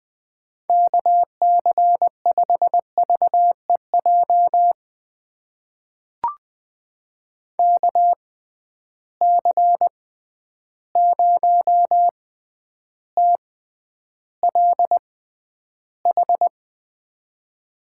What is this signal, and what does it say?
Morse code, heard as a keyed tone, about 20 wpm, spaced as for 10 wpm, 700 Hz. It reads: KC5VEJ KC0TLH
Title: USA Call Signs - 6 Characters - 6x Character Spacing - 20wpm